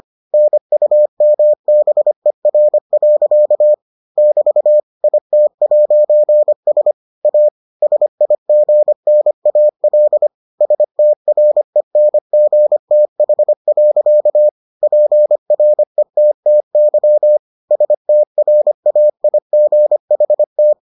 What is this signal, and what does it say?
Morse code, heard as a keyed tone, about 25 wpm, 600 Hz. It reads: NUMBER. <BT> IT'S A SIGNAL STRENGTH. PRETTY STRAIGHT